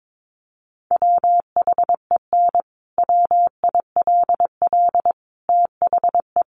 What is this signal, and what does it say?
Morse code, heard as a keyed tone, about 22 wpm, 700 Hz. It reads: WHEN WILL THE
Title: Sentences from Top 1400 Words 22wpm